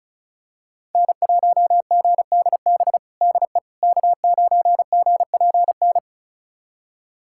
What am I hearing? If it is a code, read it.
Morse code, heard as a keyed tone, about 36 wpm, 700 Hz. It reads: N1GDB DE K9GPN
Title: USA Callsign DE Callsign - 5 Characters - 35wpm